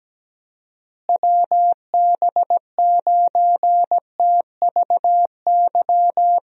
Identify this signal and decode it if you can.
Morse code, heard as a keyed tone, about 17 wpm, 700 Hz. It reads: WB9TVY
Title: USA Call Signs - 6 Characters - Repeated 2x - 17wpm